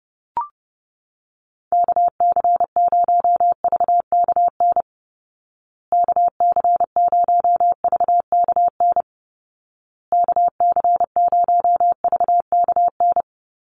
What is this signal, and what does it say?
Morse code, heard as a keyed tone, about 30 wpm, 700 Hz. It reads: KC0VKN KC0VKN KC0VKN
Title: Call Signs Repeated 2x - 30wpm